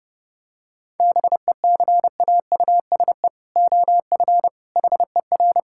Morse code, heard as a keyed tone, about 30 wpm, 700 Hz. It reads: BECAUSE OF HER
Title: Sentences from Top 1000 Words - Review Part 5 - 30wpm